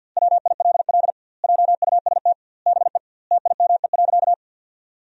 Morse code, heard as a keyed tone, about 50 wpm, 700 Hz. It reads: WILL JUST BE TIME.
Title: 2x Word-Spacing - Sentences from Top 200 Words - 50wpm